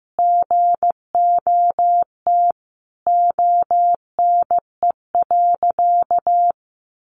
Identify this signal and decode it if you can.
Morse code, heard as a keyed tone, about 15 wpm, 700 Hz. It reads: GOT ONE.
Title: Sentences from Top 1300 Words 15wpm